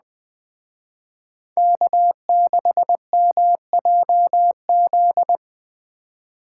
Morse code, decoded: K6MJZ

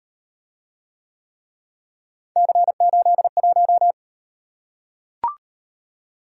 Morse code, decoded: C81

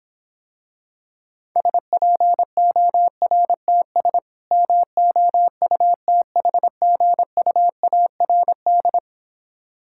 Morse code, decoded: SPORTS MOUTHGUARD

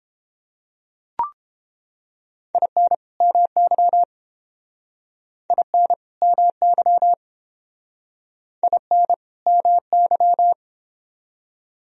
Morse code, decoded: IN MY IN MY IN MY

